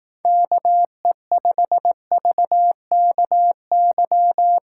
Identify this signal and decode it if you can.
Morse code, heard as a keyed tone, about 18 wpm, 700 Hz. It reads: KE5VKY
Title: Speed Racing - USA 2x3 Call Signs 15wpm